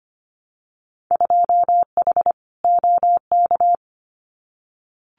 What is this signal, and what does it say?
Morse code, heard as a keyed tone, about 25 wpm, 700 Hz. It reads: 2H OK
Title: Winter Field Day Exchanges Encore 25wpm